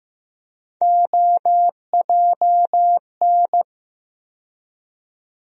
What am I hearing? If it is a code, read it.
Morse code, heard as a keyed tone, about 15 wpm, 700 Hz. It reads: OJN
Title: USA Call Sign Postfixes - 3 Letters 15wpm